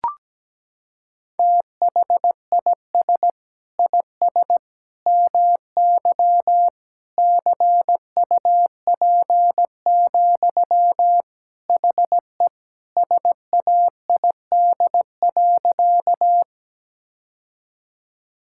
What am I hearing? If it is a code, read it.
Morse code, heard as a keyed tone, about 17 wpm, 700 Hz. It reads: THIS IS MY CUP, HE SAID.
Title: Sentences from Top 1600 Words 17wpm